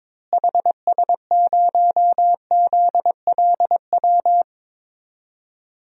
Morse code, heard as a keyed tone, about 21 wpm, 700 Hz. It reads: HS0ZLW
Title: DXCC-100 Call Signs - 6 Characters - Repeated 2x - 22wpm